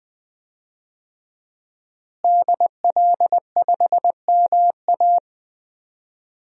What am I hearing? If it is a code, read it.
Morse code, heard as a keyed tone, about 20 wpm, 700 Hz. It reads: DL5MA
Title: DXCC-100 Call Signs - 5 Characters - Repeated 3x - 20wpm